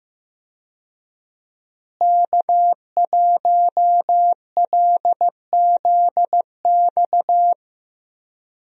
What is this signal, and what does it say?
Morse code, heard as a keyed tone, about 15 wpm, 700 Hz. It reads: K1LZX
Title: USA Call Signs - 5 Characters - Repeated 2x - 15wpm